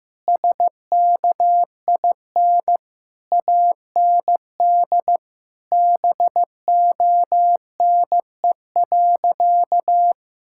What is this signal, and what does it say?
Morse code, heard as a keyed tone, about 15 wpm, 700 Hz. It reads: SKIN AND BONE.
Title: Sentences from Top 1600 Words 15wpm